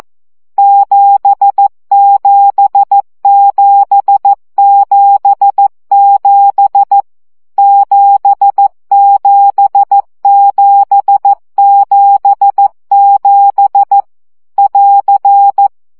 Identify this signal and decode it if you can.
Morse code, heard as a keyed tone, about 14 wpm, 800 Hz. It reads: 77777 77777 <AR>
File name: CW gyakorlas 7 12 baud